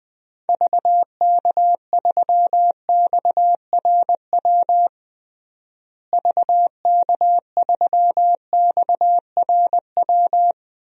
Morse code, heard as a keyed tone, about 20 wpm, 700 Hz. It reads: VK3XRW VK3XRW
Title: DXCC-100 Call Signs - 6 Characters - Repeated 2x - 20wpm